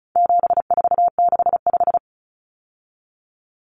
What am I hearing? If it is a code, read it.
Morse code, heard as a keyed tone, about 34 wpm, 700 Hz. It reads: 7465
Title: Numbers - 4-Digit 35wpm